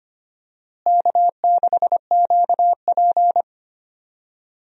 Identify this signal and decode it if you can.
Morse code, heard as a keyed tone, about 25 wpm, 700 Hz. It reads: K6QP
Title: USA Call Signs - 4 Characters - Repeated 2x - 25wpm